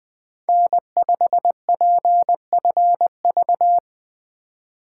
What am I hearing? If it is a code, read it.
Morse code, heard as a keyed tone, about 20 wpm, 700 Hz. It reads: N5PFV
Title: USA Call Signs - 2x Character Spacing - 20wpm